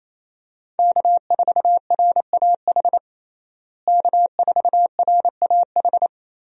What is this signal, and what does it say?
Morse code, heard as a keyed tone, about 28 wpm, 700 Hz. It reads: K4RAH K4RAH
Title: USA Call Signs - 5 Characters - Repeated 3x - 28wpm